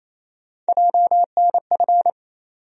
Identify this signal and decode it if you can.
Morse code, heard as a keyed tone, about 28 wpm, 700 Hz. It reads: JNF
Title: USA Call Sign Postfixes - 3 Letters 28wpm